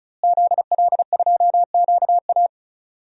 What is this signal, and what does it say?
Morse code, heard as a keyed tone, about 36 wpm, 700 Hz. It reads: ZL2QA
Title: DXCC-100 Call Signs - 5 Characters - Repeated 3x - 35wpm